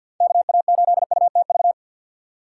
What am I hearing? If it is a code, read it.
Morse code, heard as a keyed tone, about 47 wpm, 700 Hz. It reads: KA8UTV
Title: USA Call Signs - 6x Character Spacing - 50wpm